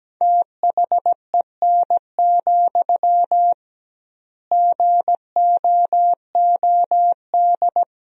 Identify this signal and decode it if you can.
Morse code, heard as a keyed tone, about 17 wpm, 700 Hz. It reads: THEN, GOOD
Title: 2x Word-Spacing - Sentences from Top 300 Words - 17wpm